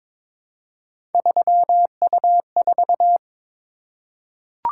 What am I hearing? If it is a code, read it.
Morse code, heard as a keyed tone, about 22 wpm, 700 Hz. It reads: 3U4
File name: practice / 22wpm / Three Letter-Number Combinations - 22wpm